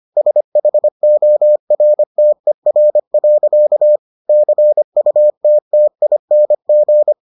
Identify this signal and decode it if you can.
Morse code, heard as a keyed tone, about 25 wpm, 600 Hz. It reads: SHORTER. CUTTING